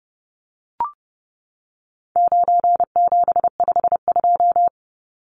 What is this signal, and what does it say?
Morse code, heard as a keyed tone, about 30 wpm, 700 Hz. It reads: 9752
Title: Numbers - 4-Digit 30wpm